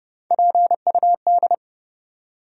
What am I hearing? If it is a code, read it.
Morse code, heard as a keyed tone, about 30 wpm, 700 Hz. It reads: PUD
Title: USA Call Sign Postfixes - 3 Letters 30wpm